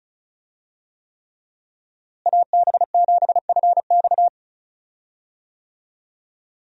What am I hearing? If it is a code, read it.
Morse code, heard as a keyed tone, about 36 wpm, 700 Hz. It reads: AB7FX
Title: USA Call Signs - 5 Characters - Repeated 3x - 35wpm